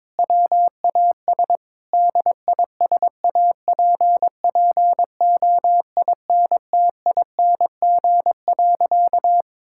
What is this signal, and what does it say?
Morse code, heard as a keyed tone, about 22 wpm, 700 Hz. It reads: WAS DISAPPOINTING.
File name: practice / 22wpm / Sentences with 4-Word Phrases - 22wpm